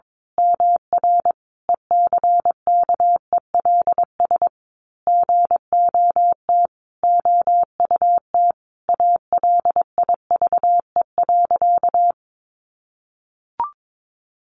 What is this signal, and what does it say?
Morse code, heard as a keyed tone, about 22 wpm, 700 Hz. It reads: MR ECKELS GOT OUT ALIVE.
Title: A Sound Of Thunder 22wpm